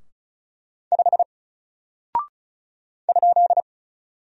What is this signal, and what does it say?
Morse code, heard as a keyed tone, about 36 wpm, 700 Hz. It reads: 5 ?
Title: Single Character (Letter:Number:Punctuation:Prosign) 35wpm